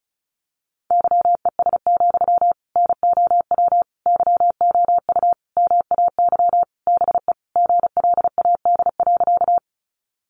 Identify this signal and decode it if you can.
Morse code, heard as a keyed tone, about 34 wpm, 700 Hz. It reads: YES, NOW YOU MAY BE GLAD.
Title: Sentences from Top 800 Words 35wpm